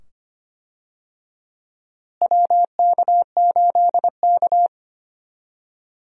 Morse code, heard as a keyed tone, about 25 wpm, 700 Hz. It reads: WK8K
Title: USA Call Signs - 4 Characters - Repeated 2x - 25wpm